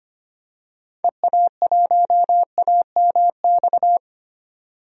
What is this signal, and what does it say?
Morse code, heard as a keyed tone, about 25 wpm, 700 Hz. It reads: EA1AMX
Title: DXCC-100 Call Signs - 6 Characters - 25wpm